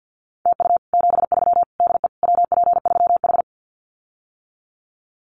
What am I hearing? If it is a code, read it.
Morse code, heard as a keyed tone, about 48 wpm, 700 Hz. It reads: TU 73 DE WP3H
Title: Common QSO Phrases Encore 50wpm